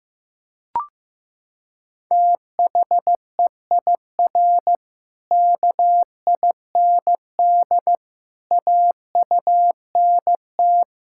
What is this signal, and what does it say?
Morse code, heard as a keyed tone, about 15 wpm, 700 Hz. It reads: THEIR KIND AUNT
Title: Pangram Challenge v2 - 15wpm